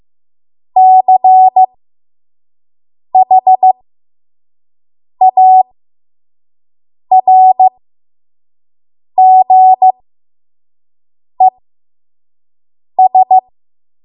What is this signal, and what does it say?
Morse code, heard as a keyed tone, about 15 wpm, spaced as for 5 wpm, 750 Hz. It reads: CHARGES
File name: Morse / 05WPM